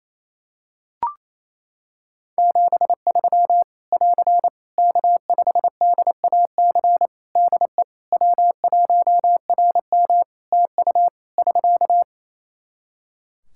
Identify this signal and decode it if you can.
Morse code, heard as a keyed tone, about 28 wpm, 700 Hz. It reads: 73 <AR> K5DAC DE W1RM TU <SK>
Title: Practice QSO - Exchange 4 - Ending - 28wpm